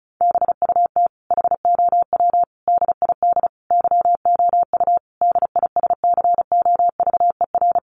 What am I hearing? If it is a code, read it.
Morse code, heard as a keyed tone, about 34 wpm, 700 Hz. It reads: BUT HOW DID YOU DISCOVER
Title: Sentences from Top 1400 Words 35wpm